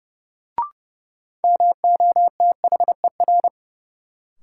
Morse code, heard as a keyed tone, about 30 wpm, 700 Hz. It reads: MOTHER
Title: Sentences from Top 200 Words - New Word Drill - 30wpm